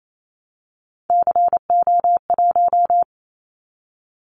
Morse code, heard as a keyed tone, about 28 wpm, 700 Hz. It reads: CO1
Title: Callsign Prefixes for Top 100 DXCC - 28wpm